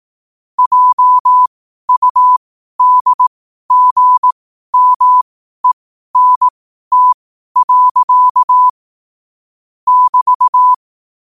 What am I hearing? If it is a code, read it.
Morse code, heard as a keyed tone, about 18 wpm, spaced as for 12 wpm, 1000 Hz. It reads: JUDGMENT. <BT>
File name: 12wpm